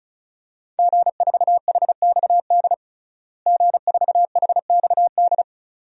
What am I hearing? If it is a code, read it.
Morse code, heard as a keyed tone, about 34 wpm, 700 Hz. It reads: G4HXD G4HXD
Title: DXCC-100 Call Signs - 5 Characters - Repeated 2x - 35wpm